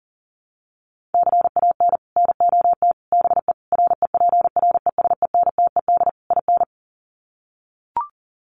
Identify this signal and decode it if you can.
Morse code, heard as a keyed tone, about 40 wpm, 700 Hz. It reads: CAN NOT BE REPRESENTED IN
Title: Sets of 5 Words 40wpm